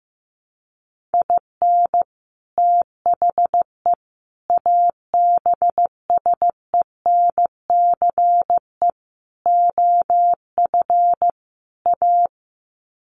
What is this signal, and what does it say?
Morse code, heard as a keyed tone, about 15 wpm, 700 Hz. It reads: IN THE ABSENCE OF A
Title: Sets of 5 Words Encore 15wpm